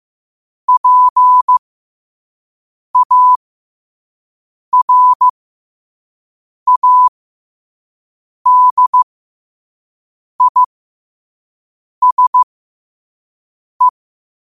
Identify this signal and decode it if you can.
Morse code, heard as a keyed tone, about 15 wpm, spaced as for 5 wpm, 1000 Hz. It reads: PARADISE
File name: qotdfiles / 5wpm